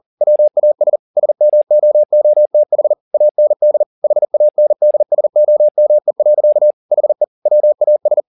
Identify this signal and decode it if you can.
Morse code, heard as a keyed tone, about 40 wpm, 600 Hz. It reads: WAS SMOOTH AND HANDSOME. HE WAS
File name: the old man and the sea 2 40wpm